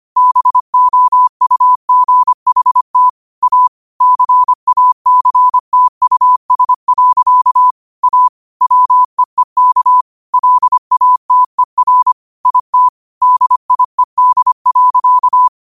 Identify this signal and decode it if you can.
Morse code, heard as a keyed tone, about 25 wpm, 1000 Hz. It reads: BOUGHT A CACTUS. A WEEK LATER IT DIED.
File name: qotdfiles / 25wpm